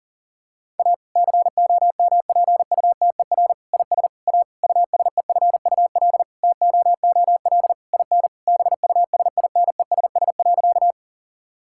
Morse code, heard as a keyed tone, about 40 wpm, 700 Hz. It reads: A COMPUTER IS A USEFUL TOOL IN BUSINESS.
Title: Sentences from Top 1000 Words - Review Part 6 - 40wpm